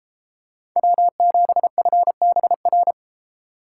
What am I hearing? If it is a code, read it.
Morse code, heard as a keyed tone, about 32 wpm, 700 Hz. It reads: W7FBR